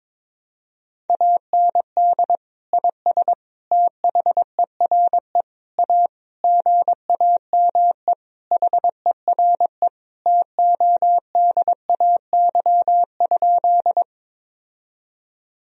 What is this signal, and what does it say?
Morse code, heard as a keyed tone, about 22 wpm, 700 Hz. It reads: AND IS THERE A GAME HERE TODAY?
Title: Sentences from Top 1000 Words - Review Part 3 - 22wpm